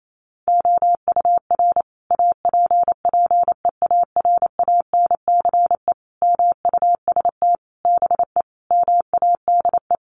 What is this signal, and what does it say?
Morse code, heard as a keyed tone, about 28 wpm, 700 Hz. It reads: OUR APPEARANCE MUST BE MADE
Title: Sentences from Top 1100 Words 28wpm